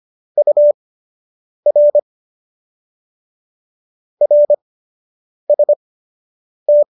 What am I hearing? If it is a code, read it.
Morse code, heard as a keyed tone, about 25 wpm, spaced as for 8 wpm, 590 Hz. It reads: UR RST